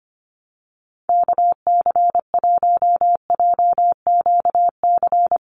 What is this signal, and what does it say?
Morse code, heard as a keyed tone, about 25 wpm, 700 Hz. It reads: KC1JQC
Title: USA Call Signs 25wpm